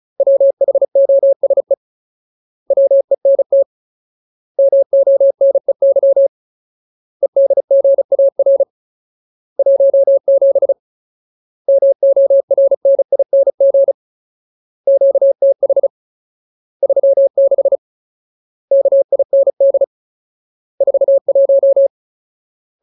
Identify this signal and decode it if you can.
Morse code, heard as a keyed tone, about 34 wpm, 550 Hz. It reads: WHOSE WENT MONEY EDGAR 17 MORNING QTH 36 KIND 41
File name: Ultimate Word List / 35WPM Ultimate